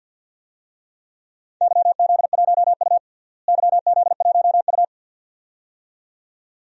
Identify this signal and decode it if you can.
Morse code, heard as a keyed tone, about 52 wpm, 700 Hz. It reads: YZ1U YZ1U
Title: DXCC-100 Call Signs - 4 Characters - Repeated 2x - 50wpm